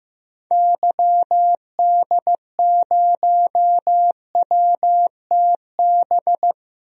Morse code, heard as a keyed tone, about 15 wpm, 700 Hz. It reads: YD0WTB
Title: DXCC-100 Call Signs - 6 Characters - 15wpm